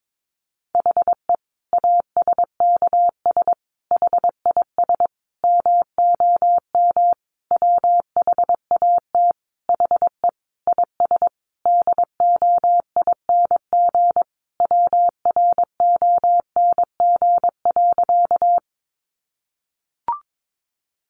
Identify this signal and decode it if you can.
Morse code, heard as a keyed tone, about 22 wpm, 700 Hz. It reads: HE ASKS HIS MOM WHAT HE IS DOING WRONG.
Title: Easy Phrases 22wpm